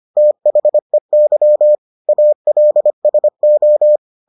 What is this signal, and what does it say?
Morse code, heard as a keyed tone, about 25 wpm, 600 Hz. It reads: THEY ALSO